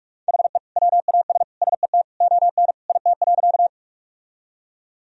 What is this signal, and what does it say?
Morse code, heard as a keyed tone, about 42 wpm, 700 Hz. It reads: HE WAS SET ON IT.
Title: Sentences from Top 1000 Words - Review Part 1 - 45wpm